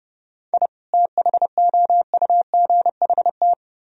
Morse code, heard as a keyed tone, about 29 wpm, 700 Hz. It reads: I THOUGHT